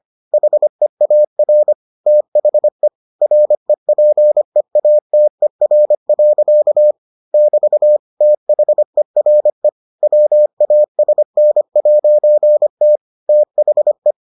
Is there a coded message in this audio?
Yes: Morse code from 0.3 to 14.1 s, heard as a keyed tone, about 25 wpm, 600 Hz. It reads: HEAR THE REPEATER. <BT> THERE WASN'T THE